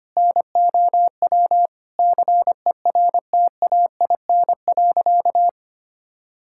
Morse code, heard as a keyed tone, about 25 wpm, 700 Hz. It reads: NOW CERTAIN.
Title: Sentences from Top 1000 Words - Review Part 5 - 25wpm